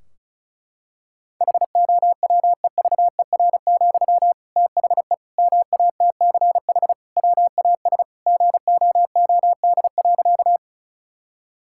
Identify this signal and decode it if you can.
Morse code, heard as a keyed tone, about 36 wpm, 700 Hz. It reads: HOWEVER, THE MATCH WAS GOOD.
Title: Sentences from Top 1600 Words 35wpm